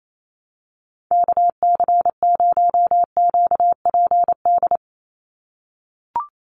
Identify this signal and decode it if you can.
Morse code, heard as a keyed tone, about 28 wpm, 700 Hz. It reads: KC0QPD
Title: USA Call Signs 28wpm